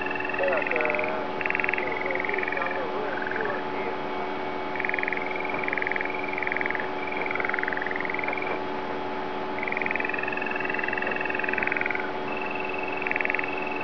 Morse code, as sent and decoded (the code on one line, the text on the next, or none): none